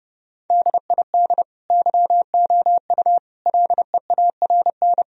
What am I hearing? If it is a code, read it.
Morse code, heard as a keyed tone, about 30 wpm, 700 Hz. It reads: DID YOU LEARN
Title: Sentences from Top 1000 Words - Review Part 4 - 30wpm